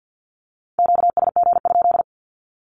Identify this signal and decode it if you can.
Morse code, heard as a keyed tone, about 48 wpm, 700 Hz. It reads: QSG?
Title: Q-Codes 50wpm